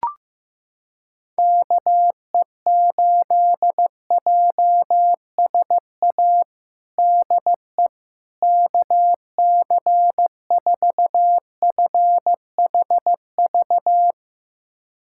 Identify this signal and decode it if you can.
Morse code, heard as a keyed tone, about 16 wpm, 700 Hz. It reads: KE8JSA DE KC4FHV
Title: USA Callsign DE Callsign - 6 Characters - 15wpm